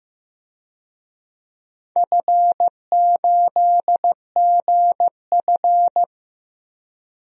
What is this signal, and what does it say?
Morse code, heard as a keyed tone, about 15 wpm, 700 Hz. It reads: F8GF
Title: DXCC-100 Call Signs - 4 Characters - Repeated 2x - 15wpm